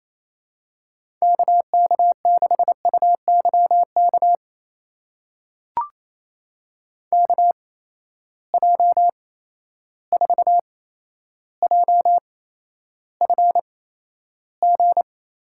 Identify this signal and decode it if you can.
Morse code, heard as a keyed tone, about 28 wpm, spaced as for 12 wpm, 700 Hz. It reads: KK6UYK KJ4JFG
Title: USA Call Signs - 6 Characters - 8x Character Spacing - 28wpm